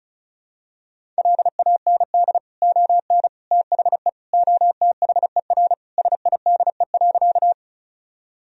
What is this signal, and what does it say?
Morse code, heard as a keyed tone, about 36 wpm, 700 Hz. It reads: LAND ON THE OTHER SIDE.
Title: Sentences from Top 1000 Words - Review Part 1 - 35wpm